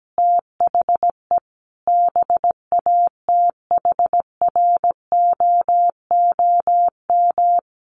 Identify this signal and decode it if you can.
Morse code, heard as a keyed tone, about 17 wpm, 700 Hz. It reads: THE BATHROOM
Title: Sets of 5 Words Encore 17wpm